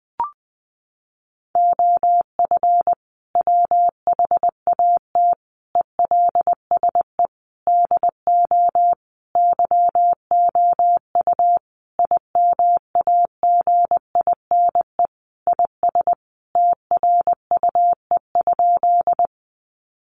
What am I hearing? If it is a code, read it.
Morse code, heard as a keyed tone, about 20 wpm, 700 Hz. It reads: OF WHAT ELSE DO YOU IMAGINE IS TRUE?
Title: Sentences from Top 1400 Words 20wpm